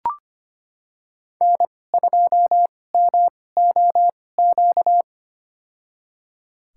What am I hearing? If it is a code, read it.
Morse code, heard as a keyed tone, about 25 wpm, spaced as for 18 wpm, 700 Hz. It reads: N2MOQ
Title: USA Call Signs - 5 Characters - 2x Character Spacing - 25wpm